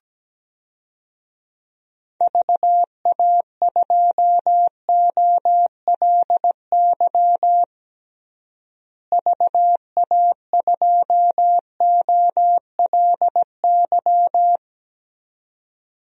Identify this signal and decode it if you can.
Morse code, heard as a keyed tone, about 17 wpm, 700 Hz. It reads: VA2OLY VA2OLY
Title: DXCC-100 Call Signs - 6 Characters - Repeated 2x - 17wpm